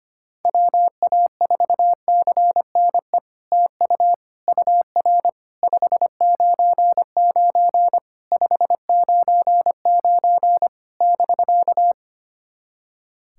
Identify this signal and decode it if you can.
Morse code, heard as a keyed tone, about 25 wpm, 700 Hz. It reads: WA4CNE TU UR 599 599 <BK>
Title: SOTA QSO Practice 25wpm